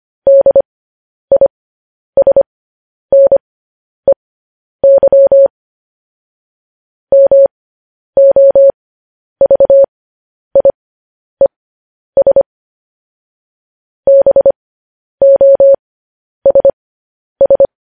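Morse code, heard as a keyed tone, about 25 wpm, spaced as for 10 wpm, 550 Hz. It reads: DISNEY MOVIES BOSS
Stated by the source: Morse news headlines sent at 10 wpm.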